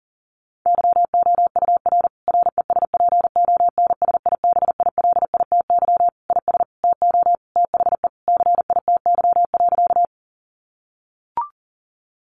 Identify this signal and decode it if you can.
Morse code, heard as a keyed tone, about 40 wpm, 700 Hz. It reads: YOUR RESPONSIBILITY IS TO THE CITY.
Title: Sentences from Top 1200 Words 40wpm